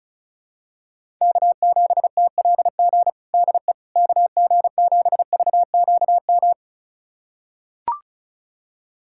Morse code, heard as a keyed tone, about 36 wpm, 700 Hz. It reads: K7TLG DE KG7VQM
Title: USA Callsign DE Callsign 35wpm